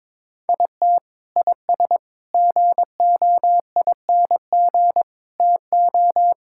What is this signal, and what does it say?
Morse code, heard as a keyed tone, about 22 wpm, 700 Hz. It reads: IT IS GOING TO
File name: practice / 22wpm / Sentences with 4-Word Phrases - 22wpm